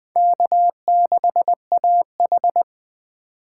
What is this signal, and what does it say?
Morse code, heard as a keyed tone, about 23 wpm, 700 Hz. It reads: K6AH
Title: Speed Racing - USA 1x2 Call Signs 20wpm